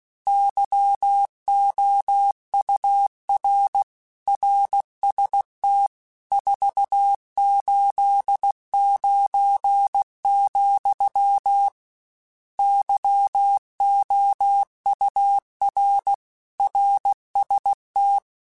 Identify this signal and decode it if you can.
Morse code, heard as a keyed tone, about 16 wpm, 800 Hz. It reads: YOUR RST 489, YOUR RST
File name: RSTs / RST-mp3s / Your RSTs @15